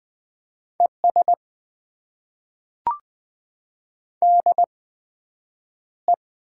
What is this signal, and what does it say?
Morse code, heard as a keyed tone, about 20 wpm, spaced as for 9 wpm, 700 Hz. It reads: ES DE
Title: Top 90 QSO Elements - 8x Character Spacing - 20wpm